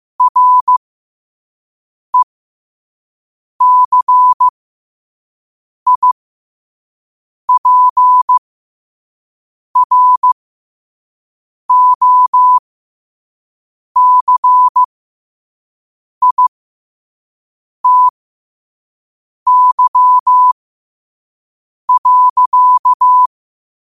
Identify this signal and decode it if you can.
Morse code, heard as a keyed tone, about 15 wpm, spaced as for 5 wpm, 1000 Hz. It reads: RECIPROCITY.
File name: qotdfiles / 5wpm